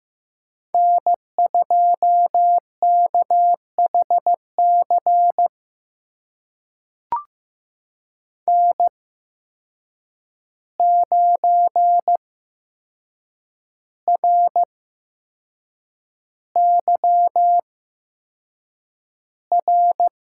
Morse code, heard as a keyed tone, about 15 wpm, spaced as for 6 wpm, 700 Hz. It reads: N2KHC N9RYR
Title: USA Call Signs - 5 Characters - 8x Character Spacing - 15wpm